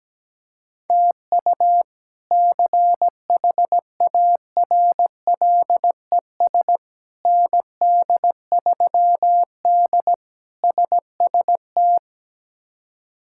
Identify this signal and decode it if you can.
Morse code, heard as a keyed tone, about 17 wpm, 700 Hz. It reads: TU CHARLES ND3D SST
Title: K1USN SST QSO Practice - 17wpm